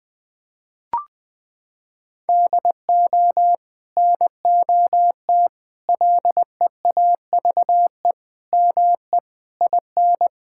Morse code, heard as a keyed tone, about 21 wpm, 700 Hz. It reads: DO NOT LEAVE ME IN
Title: Sentences from Top 1600 Words 20wpm